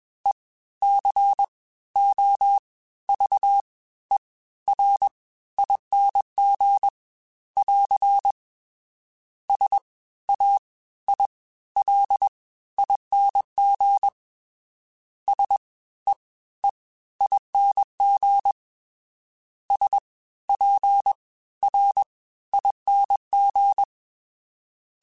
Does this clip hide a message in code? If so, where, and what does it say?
Yes: Morse code from 0.3 to 23.8 s, heard as a keyed tone, about 21 wpm, spaced as for 14 wpm, 800 Hz. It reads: ECOVERING <AR> SAILING SEEING SPRING